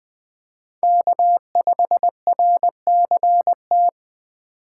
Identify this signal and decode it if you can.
Morse code, heard as a keyed tone, about 20 wpm, 700 Hz. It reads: K5RCT
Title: Speed Racing - USA Call Signs 17wpm